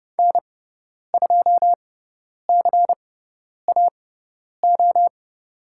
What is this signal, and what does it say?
Morse code, heard as a keyed tone, about 30 wpm, spaced as for 10 wpm, 700 Hz. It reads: N2CAO
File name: Lesson 17g - Letter u - Call Signs 30wpm